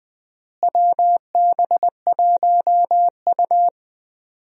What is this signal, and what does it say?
Morse code, heard as a keyed tone, about 20 wpm, 690 Hz. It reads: WB1U
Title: USA Call Signs - 4 Characters - Repeated 3x - 20wpm